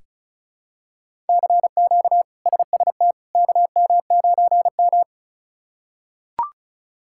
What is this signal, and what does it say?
Morse code, heard as a keyed tone, about 36 wpm, 700 Hz. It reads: CQ SST KM9M
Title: K1USN SST QSO Practice - 35wpm